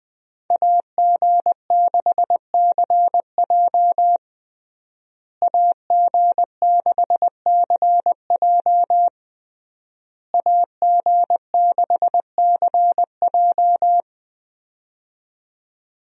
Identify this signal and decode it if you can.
Morse code, heard as a keyed tone, about 20 wpm, 700 Hz. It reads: AG6CJ AG6CJ AG6CJ
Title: USA Call Signs - 5 Characters - Repeated 3x - 20wpm